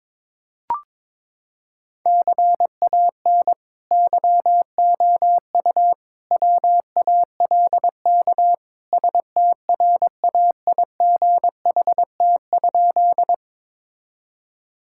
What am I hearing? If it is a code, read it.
Morse code, heard as a keyed tone, about 22 wpm, 700 Hz. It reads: CAN YOU WALK STRAIGHT?
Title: Sentences from Top 1000 Words - Review Part 6 - 22wpm